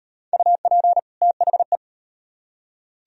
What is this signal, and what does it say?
Morse code, heard as a keyed tone, about 36 wpm, 700 Hz. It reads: UP THE